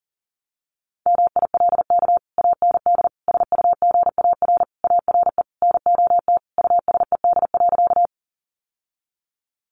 Morse code, heard as a keyed tone, about 40 wpm, 700 Hz. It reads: MILK AND SUGAR ARE NOT USED.
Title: Sentences from Top 1500 Words 40wpm